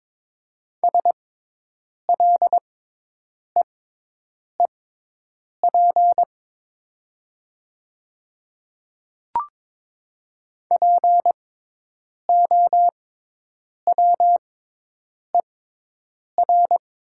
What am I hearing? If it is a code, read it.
Morse code, heard as a keyed tone, about 22 wpm, spaced as for 8 wpm, 700 Hz. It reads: SLEEP POWER
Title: 6x Character Spacing - Top 500 Words - 22wpm